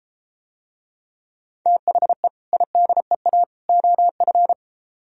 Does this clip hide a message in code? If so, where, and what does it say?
Yes: Morse code from 1.7 to 4.5 s, heard as a keyed tone, about 33 wpm, 700 Hz. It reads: THE IDEA OF